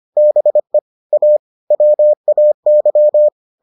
Morse code, heard as a keyed tone, about 25 wpm, 600 Hz. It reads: BE A WAY